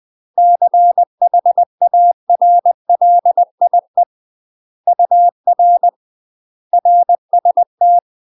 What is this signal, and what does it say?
Morse code, heard as a keyed tone, about 20 wpm, 700 Hz. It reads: CHARLIE UR RST
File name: UK MASTER QSOs TRAINING / 20WPM / UK MASTER QSOs 0062